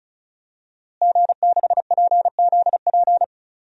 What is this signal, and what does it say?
Morse code, heard as a keyed tone, about 36 wpm, 700 Hz. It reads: G6PZP
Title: DXCC-100 Call Signs - 5 Characters - Repeated 2x - 35wpm